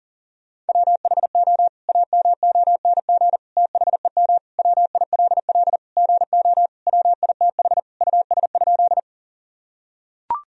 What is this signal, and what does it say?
Morse code, heard as a keyed tone, about 40 wpm, 700 Hz. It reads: WHO AMONG THEM WILL GO WITH US?
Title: Sentences from Top 1000 Words - Review Part 3 - 40wpm